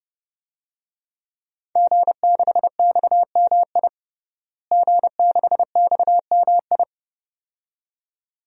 Morse code, heard as a keyed tone, about 30 wpm, 700 Hz. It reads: G6XMI G6XMI
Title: DXCC-100 Call Signs - 5 Characters - Repeated 2x - 30wpm